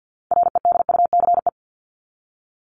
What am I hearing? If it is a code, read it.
Morse code, heard as a keyed tone, about 50 wpm, 700 Hz. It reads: REDUCE
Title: ED Suffix - Top 100 Words with Base Words 50wpm